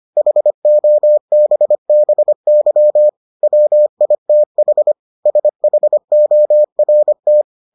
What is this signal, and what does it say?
Morse code, heard as a keyed tone, about 25 wpm, 600 Hz. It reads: HOBBY WITH SHORT